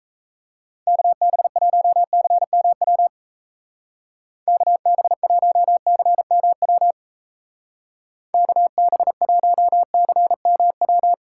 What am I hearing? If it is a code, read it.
Morse code, heard as a keyed tone, about 35 wpm, 700 Hz. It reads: KB1CMW KB1CMW KB1CMW
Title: Speed Racing - USA Call Signs 28wpm